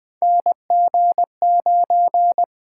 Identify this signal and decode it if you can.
Morse code, heard as a keyed tone, about 20 wpm, 700 Hz. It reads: NG9
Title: USA Call Signs - 4 Characters - 8x Character Spacing - 20wpm